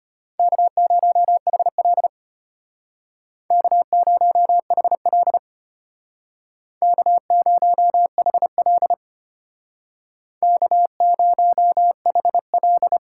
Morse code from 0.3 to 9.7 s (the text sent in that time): K0HL K0HL K0HL